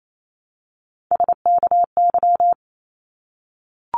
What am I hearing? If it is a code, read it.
Morse code, heard as a keyed tone, about 28 wpm, 700 Hz. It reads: SKY
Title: Multi-letter Phonemes 28wpm